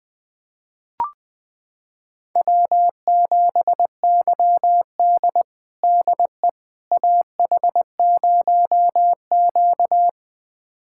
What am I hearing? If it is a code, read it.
Morse code, heard as a keyed tone, about 21 wpm, 700 Hz. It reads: W7YD DE AH0Q
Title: USA Callsign DE Callsign - 4 Characters - 20wpm